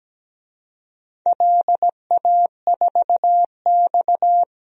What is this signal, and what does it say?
Morse code, heard as a keyed tone, about 17 wpm, 700 Hz. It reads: LA4X
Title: DXCC-100 Call Signs - 4 Characters - 17wpm